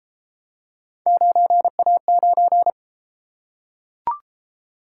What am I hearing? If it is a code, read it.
Morse code, heard as a keyed tone, about 32 wpm, 700 Hz. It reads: 9A9